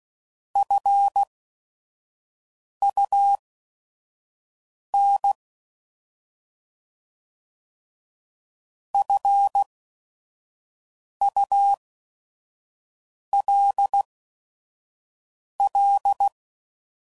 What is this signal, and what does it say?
Morse code, heard as a keyed tone, about 16 wpm, spaced as for 5 wpm, 800 Hz. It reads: FUN FULL
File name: Prefixes / By First Letter / Fs